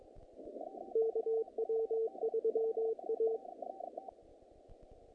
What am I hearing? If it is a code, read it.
Morse code, heard as a keyed tone, about 22 wpm, 450 Hz, with radio noise. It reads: KW3A
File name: contests / naqp-cw-k5tr-op-k5pi- -2100